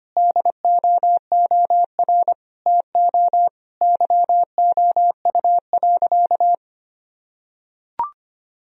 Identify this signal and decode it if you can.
Morse code, heard as a keyed tone, about 25 wpm, 700 Hz. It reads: DOOR TO YOU.
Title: Sentences from Top 1000 Words - Review Part 1 - 25wpm